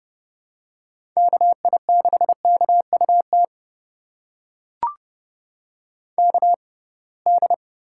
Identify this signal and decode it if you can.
Morse code, heard as a keyed tone, about 32 wpm, spaced as for 23 wpm, 700 Hz. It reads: KI6KUT KD
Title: USA Call Signs - 6 Characters - 6x Character Spacing - 30wpm